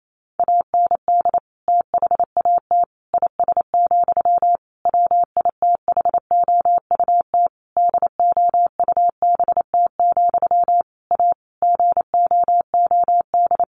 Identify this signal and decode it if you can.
Morse code, heard as a keyed tone, about 28 wpm, 700 Hz. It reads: AND THAT IS, WITHOUT DOUBT, A GOOD
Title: Sentences from Top 1500 Words 28wpm